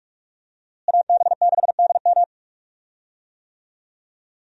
Morse code, heard as a keyed tone, about 41 wpm, 700 Hz. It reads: AB6DM
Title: USA Call Signs - 5 Characters - Repeated 2x - 45wpm